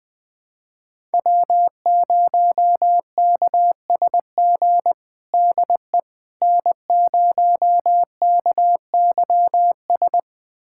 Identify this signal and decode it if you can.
Morse code, heard as a keyed tone, about 21 wpm, 700 Hz. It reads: W0KSG DE N0KYS
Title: USA Callsign DE Callsign - 5 Characters - 20wpm